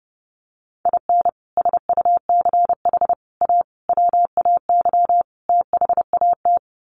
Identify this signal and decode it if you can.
Morse code, heard as a keyed tone, about 30 wpm, 700 Hz. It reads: IN SUCH A WAY THAT
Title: Sets of 5 Words 30wpm